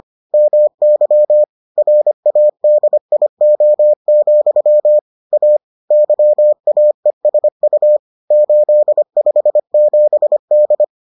Morse code, heard as a keyed tone, about 25 wpm, 600 Hz. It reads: MY RADIO, A YAESU 857D